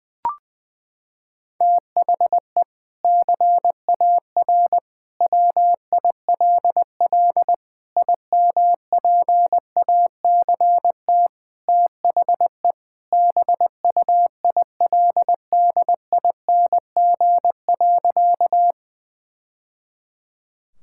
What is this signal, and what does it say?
Morse code, heard as a keyed tone, about 21 wpm, 700 Hz. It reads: THE CAR WILL IMPACT THE BUILDING.
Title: Sentences from Top 1000 Words - Review Part 6 - 20wpm